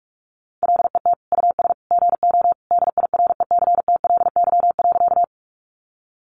Morse code, heard as a keyed tone, about 44 wpm, 700 Hz. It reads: LET US GO DIRECTLY.
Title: Sentences from Top 800 Words 45wpm